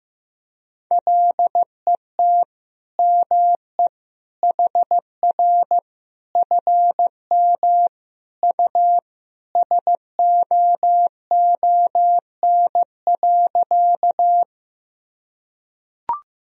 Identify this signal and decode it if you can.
Morse code, heard as a keyed tone, about 15 wpm, 700 Hz. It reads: LET ME HR FM U SOON.
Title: Sentences from Top 400 Words and Abbreviations 15wpm